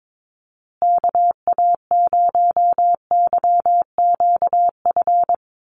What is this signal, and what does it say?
Morse code, heard as a keyed tone, about 22 wpm, 700 Hz. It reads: KA0YQF
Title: USA Call Signs Repeated 2x 22wpm